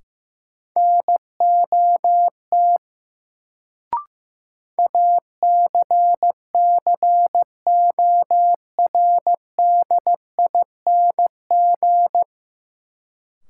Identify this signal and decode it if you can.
Morse code, heard as a keyed tone, about 15 wpm, 700 Hz. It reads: NOT ACCORDING
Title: Sets of 2 Words - New Word Drill - 15wpm